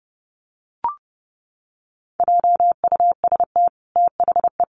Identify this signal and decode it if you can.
Morse code, heard as a keyed tone, about 30 wpm, 700 Hz. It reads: JUST THE
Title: Sentences from Top 1300 Words 30wpm